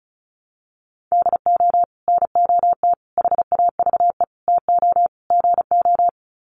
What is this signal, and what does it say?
Morse code, heard as a keyed tone, about 34 wpm, 700 Hz. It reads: DO NOT HAVE TO GO
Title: Sets of 5 Words 35wpm